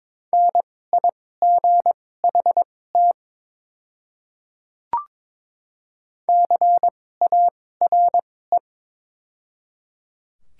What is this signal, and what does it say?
Morse code, heard as a keyed tone, about 22 wpm, spaced as for 16 wpm, 690 Hz. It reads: NIGHT CARE
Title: 2x Character Spacing - Top 500 Words - 22wpm